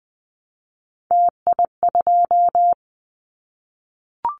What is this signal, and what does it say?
Morse code, heard as a keyed tone, about 20 wpm, 700 Hz. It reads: TI2
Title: Callsign Prefixes for Top 100 DXCC - 20wpm